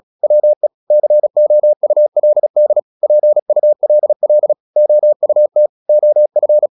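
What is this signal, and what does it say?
Morse code, heard as a keyed tone, about 36 wpm, 600 Hz. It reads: WE COULD PULL OUT OF
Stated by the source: Morse practice at 36 wpm, text of Ken Follett's novel Never, from CW med Gnister.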